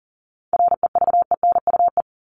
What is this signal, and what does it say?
Morse code, heard as a keyed tone, about 40 wpm, 700 Hz. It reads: REVENUE
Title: Top 1501-1750 Words 40wpm